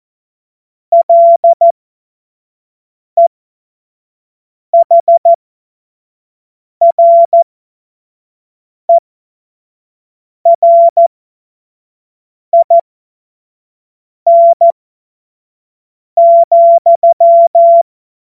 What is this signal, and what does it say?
Morse code, heard as a keyed tone, about 14 wpm, spaced as for 5 wpm, 680 Hz. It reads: LEHRERIN,
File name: Morsekurs / Witze / Witz 21. Februar 2023 25 F70